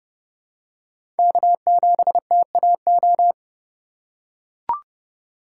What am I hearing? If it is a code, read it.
Morse code, heard as a keyed tone, about 32 wpm, 700 Hz. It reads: K7TAO
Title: USA Call Signs - 5 Characters - 3x Character Spacing - 30wpm